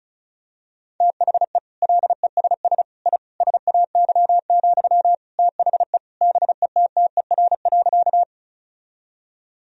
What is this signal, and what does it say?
Morse code, heard as a keyed tone, about 36 wpm, 700 Hz. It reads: THE LESS I SAY, THE BETTER.
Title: Sentences from Top 1000 Words - Review Part 2 - 35wpm